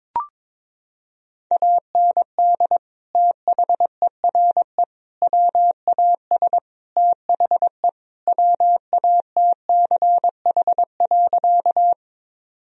Morse code, heard as a keyed tone, about 22 wpm, 700 Hz. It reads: AND THERE WAS THE WATCH.
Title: Sentences from Top 1000 Words - Review Part 1 - 22wpm